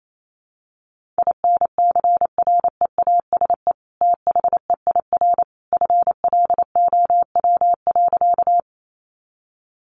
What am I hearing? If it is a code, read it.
Morse code, heard as a keyed tone, about 28 wpm, 700 Hz. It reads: INCREASE THEIR FLOW.
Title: Sentences from Top 1200 Words 28wpm